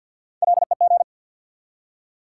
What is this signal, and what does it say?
Morse code, heard as a keyed tone, about 46 wpm, 700 Hz. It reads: LEG
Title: Three Letter Words - 50wpm